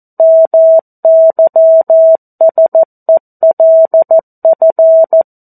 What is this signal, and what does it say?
Morse code, heard as a keyed tone, about 14 wpm, 650 Hz. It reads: MYSELF
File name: 02-Chapter-02 14 WPM